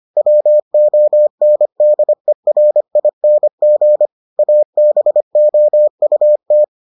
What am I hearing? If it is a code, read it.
Morse code, heard as a keyed tone, about 25 wpm, 600 Hz. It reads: WONDERING ABOUT